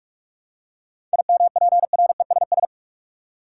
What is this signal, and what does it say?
Morse code, heard as a keyed tone, about 46 wpm, 700 Hz. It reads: IMPRESS
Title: Seven Letter Words - 45wpm